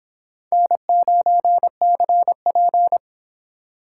Morse code, heard as a keyed tone, about 30 wpm, 700 Hz. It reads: N9CP